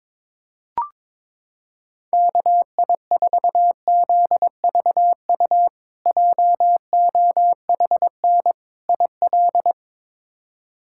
Morse code, heard as a keyed tone, about 22 wpm, 700 Hz. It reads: KI4ZVU JOHN IL
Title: K1USN SST QSO Practice - 22wpm